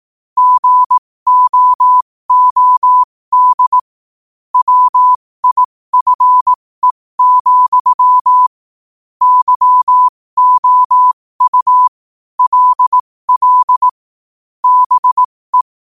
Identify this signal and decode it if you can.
Morse code, heard as a keyed tone, about 18 wpm, spaced as for 15 wpm, 1000 Hz. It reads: GOOD WIFE, YOU LL BE
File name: qotdfiles / 15wpm